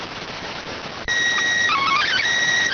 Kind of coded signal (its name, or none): SSTV (slow-scan television)